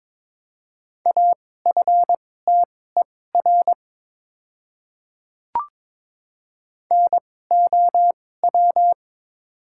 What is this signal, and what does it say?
Morse code, heard as a keyed tone, about 22 wpm, spaced as for 16 wpm, 700 Hz. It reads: AFTER NOW
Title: 2x Character Spacing - Top 200 Words - 22wpm